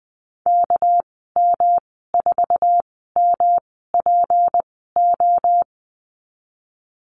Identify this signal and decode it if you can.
Morse code, heard as a keyed tone, about 20 wpm, spaced as for 15 wpm, 700 Hz. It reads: KM4MPO
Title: 213 Lesson 34g - Number 8 - Call Signs 20wpm